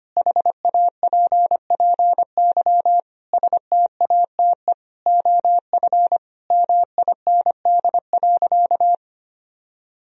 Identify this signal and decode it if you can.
Morse code, heard as a keyed tone, about 25 wpm, 700 Hz. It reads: HAPPY STATE OF MIND.
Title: Sentences from Top 1000 Words - Review Part 2 - 25wpm